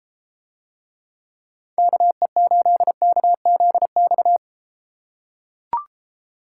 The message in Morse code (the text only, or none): KE8KZX